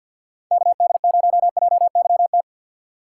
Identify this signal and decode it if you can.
Morse code, heard as a keyed tone, about 48 wpm, 700 Hz. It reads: KD0JYT
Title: USA Call Signs - 6 Characters - 3x Character Spacing - 50wpm